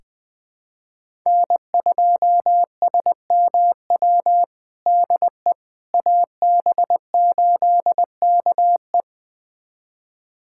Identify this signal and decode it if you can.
Morse code, heard as a keyed tone, about 20 wpm, 700 Hz. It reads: N2SMW DE AB8KE
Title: USA Callsign DE Callsign - 5 Characters - 20wpm